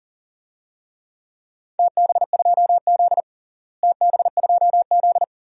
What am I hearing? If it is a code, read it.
Morse code, heard as a keyed tone, about 40 wpm, 690 Hz. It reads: TB2Z TB2Z
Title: DXCC-100 Call Signs - 4 Characters - Repeated 2x - 40wpm